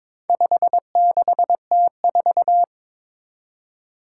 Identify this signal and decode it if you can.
Morse code, heard as a keyed tone, about 22 wpm, 700 Hz. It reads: 56T4
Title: Four Challenging Letter-Number Combinations 22wpm